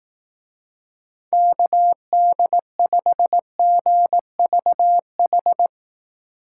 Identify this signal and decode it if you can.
Morse code, heard as a keyed tone, about 18 wpm, 700 Hz. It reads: KD5GVH
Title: Speed Racing - USA 2x3 Call Signs 15wpm